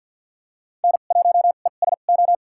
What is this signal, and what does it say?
Morse code, heard as a keyed tone, about 53 wpm, spaced as for 37 wpm, 700 Hz. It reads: N1ESO
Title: USA Call Signs - 5 Characters - 2x Character Spacing - 50wpm